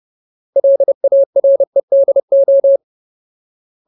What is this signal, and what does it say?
Morse code, heard as a keyed tone, about 30 wpm, 550 Hz. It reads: LAREDO
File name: Ultimate Word List / 30WPM Ultimate